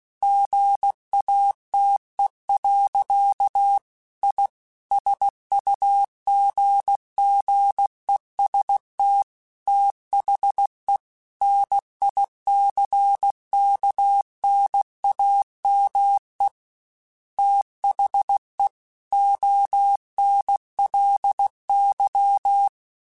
Morse code, heard as a keyed tone, about 16 wpm, 800 Hz. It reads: GATE. I SUGGEST THE NICKNAME THE ONLY
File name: Technical Subjects / Technical Subjects-mp3s / AND Gate @15